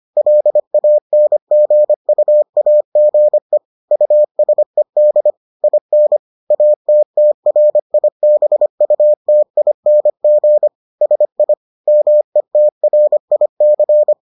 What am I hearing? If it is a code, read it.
Morse code, heard as a keyed tone, about 25 wpm, 600 Hz. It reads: LANGUAGE USED IN ATTRIBUTING SI METRIC